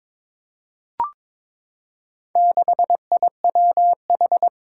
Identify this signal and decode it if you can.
Morse code, heard as a keyed tone, about 22 wpm, 700 Hz. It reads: 6IWH
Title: Four Challenging Letter-Number Combinations 22wpm